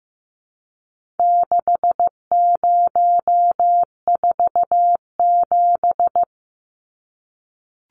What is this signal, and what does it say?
Morse code, heard as a keyed tone, about 15 wpm, 700 Hz. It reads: 6047
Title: Numbers - 4-Digit 15wpm